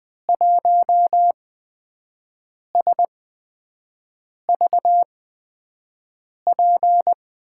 Morse code, heard as a keyed tone, about 20 wpm, spaced as for 5 wpm, 700 Hz. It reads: 1SVP
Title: USA Call Signs - 6 Characters - 8x Character Spacing - 20wpm